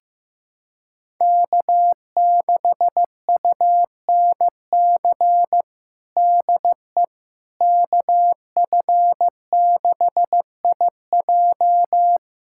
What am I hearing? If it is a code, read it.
Morse code, heard as a keyed tone, about 15 wpm, 700 Hz. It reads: K6UNC DE KF6IJ
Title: USA Callsign DE Callsign - 5 Characters - 15wpm